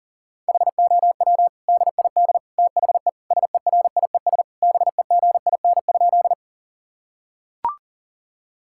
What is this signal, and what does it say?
Morse code, heard as a keyed tone, about 38 wpm, 700 Hz. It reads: HOW DID THE SERIES BEGIN?
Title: Sentences from Top 1000 Words - Review Part 5 - 40wpm